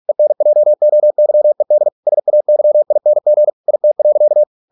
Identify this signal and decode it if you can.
Morse code, heard as a keyed tone, about 46 wpm, 600 Hz. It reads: ENJOYED SAYING IT.